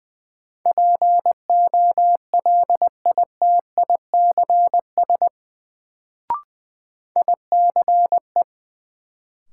Morse code, heard as a keyed tone, about 20 wpm, 700 Hz. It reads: POLITICS ICE
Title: Sentences from Top 1000 Words - New Word Drill - 20wpm